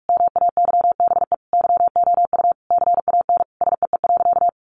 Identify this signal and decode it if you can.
Morse code, heard as a keyed tone, about 44 wpm, 700 Hz. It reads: MAYBE YOU CAN SEE.
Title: Sentences from Top 1000 Words 45wpm